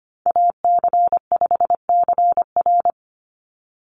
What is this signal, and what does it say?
Morse code, heard as a keyed tone, about 25 wpm, 700 Hz. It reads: AC5CR
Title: USA Call Signs 25wpm